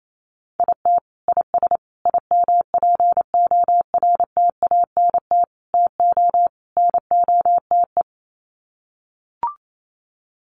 Morse code, heard as a keyed tone, about 28 wpm, 700 Hz. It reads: IT IS IMPORTANT TO NOTE
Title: Sets of 5 Words 28wpm